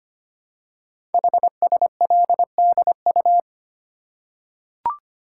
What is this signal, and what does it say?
Morse code, heard as a keyed tone, about 25 wpm, 700 Hz. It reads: HSLDU
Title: Code Groups - 5 Letter:Numbers - 25wpm